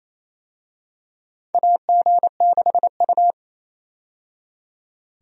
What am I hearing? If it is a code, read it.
Morse code, heard as a keyed tone, about 28 wpm, 700 Hz. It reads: AG6U
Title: USA Call Signs - 4 Characters - 28wpm